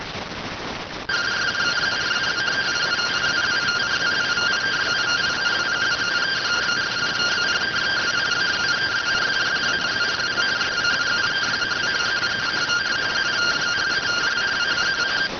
Two-tone teleprinter signal, about 170 Hz shift.